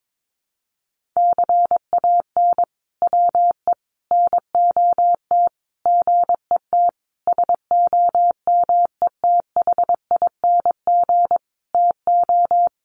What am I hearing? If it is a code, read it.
Morse code, heard as a keyed tone, about 22 wpm, 700 Hz. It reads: CAN WE NOT GET SOMETHING TO
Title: Sentences from Top 1500 Words 22wpm